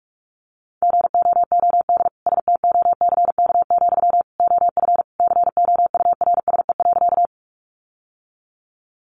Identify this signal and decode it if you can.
Morse code, heard as a keyed tone, about 44 wpm, 700 Hz. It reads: GOOD STOCK, OF COURSE.
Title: Sentences from Top 800 Words 45wpm